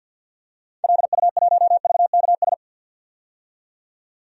Morse code, heard as a keyed tone, about 51 wpm, 700 Hz. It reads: LU1VKS
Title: DXCC-100 Call Signs - 6 Characters - 50wpm